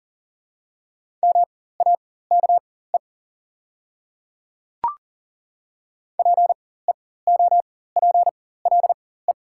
Morse code, heard as a keyed tone, about 40 wpm, spaced as for 19 wpm, 700 Hz. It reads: MAKE PEOPLE
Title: 4x Character Spacing - Top 200 Words - 40wpm